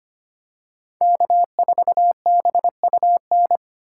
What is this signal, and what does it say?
Morse code, heard as a keyed tone, about 25 wpm, 700 Hz. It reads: K4BUN
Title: USA Call Signs - 5 Characters - 2x Character Spacing - 25wpm